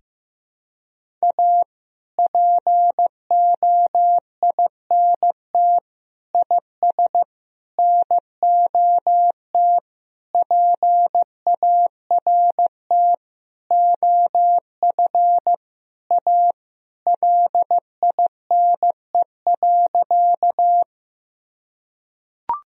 Morse code, heard as a keyed tone, about 15 wpm, 700 Hz. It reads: A POINT IS NOT PART OF A LINE.
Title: Sentences from Top 1000 Words - Review Part 1 - 15wpm